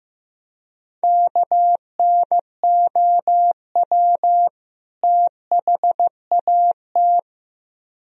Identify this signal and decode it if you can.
Morse code, heard as a keyed tone, about 15 wpm, 700 Hz. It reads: KNOW THAT
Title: Speed-Racing - Sets of 2 Words 15wpm